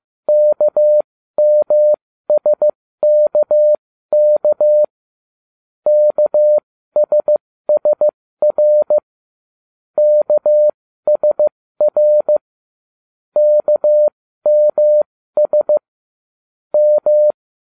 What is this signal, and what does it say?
Morse code, heard as a keyed tone, about 15 wpm, spaced as for 13 wpm, 600 Hz. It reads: KMSKK KSSR KSR KMS M